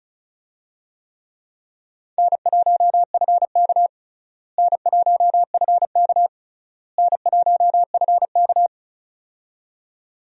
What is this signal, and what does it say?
Morse code, heard as a keyed tone, about 36 wpm, 700 Hz. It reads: N1FK N1FK N1FK
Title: USA Call Signs - 4 Characters - Repeated 3x - 35wpm